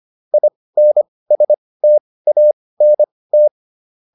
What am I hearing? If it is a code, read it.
Morse code, heard as a keyed tone, about 25 wpm, spaced as for 18 wpm, 600 Hz. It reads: INSTANT